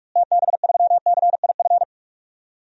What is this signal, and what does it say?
Morse code, heard as a keyed tone, about 46 wpm, 700 Hz. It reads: TB3CIF
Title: DXCC-100 Call Signs - 6 Characters - Repeated 2x - 45wpm